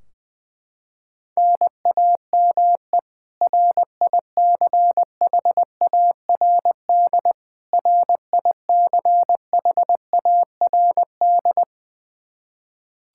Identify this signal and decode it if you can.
Morse code, heard as a keyed tone, about 20 wpm, 700 Hz. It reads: NAME RICHARD RICHARD
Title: Practice QSO - Exchange 1 - Beginning - 20wpm